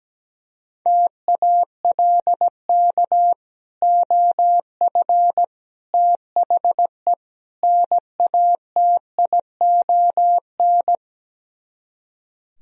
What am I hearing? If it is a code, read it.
Morse code, heard as a keyed tone, about 17 wpm, 700 Hz. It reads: TALK OF THE NATION
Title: Speed-Racing - Sets of 4 Words 17wpm